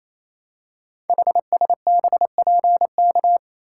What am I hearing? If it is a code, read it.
Morse code, heard as a keyed tone, about 28 wpm, 700 Hz. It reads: HSBPK
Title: Code Groups - 5 Letters - 28wpm